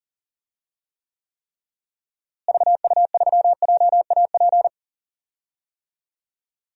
Morse code, heard as a keyed tone, about 40 wpm, 700 Hz. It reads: VU3JAP
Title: DXCC-100 Call Signs - 6 Characters - Repeated 3x - 40wpm